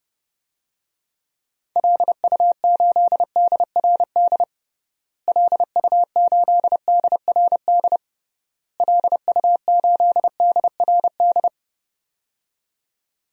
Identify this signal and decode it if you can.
Morse code, heard as a keyed tone, about 30 wpm, 700 Hz. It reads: LU8DRD LU8DRD LU8DRD
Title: DXCC-100 Call Signs - 6 Characters - Repeated 3x - 30wpm